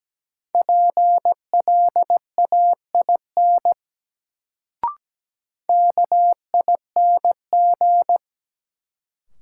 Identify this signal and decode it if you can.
Morse code, heard as a keyed tone, about 17 wpm, 700 Hz. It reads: PLAIN KING
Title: Sentences from Top 400 Words - New Word Drill - 17wpm